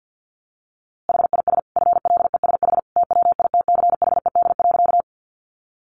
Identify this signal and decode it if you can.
Morse code, heard as a keyed tone, about 50 wpm, 700 Hz. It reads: HIS FLESH TWITCHED.
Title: A Sound Of Thunder 50wpm